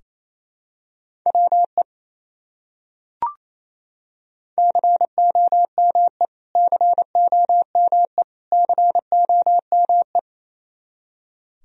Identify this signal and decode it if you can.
Morse code, heard as a keyed tone, about 27 wpm, 700 Hz. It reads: WE COME COME COME
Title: Top 100 Words Repeated 28wpm